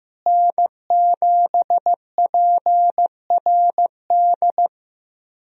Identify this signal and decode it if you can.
Morse code, heard as a keyed tone, about 15 wpm, 700 Hz. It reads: N7PRD
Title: USA Call Signs - 5 Characters - Repeated 2x - 15wpm